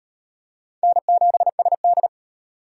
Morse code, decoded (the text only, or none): N7SD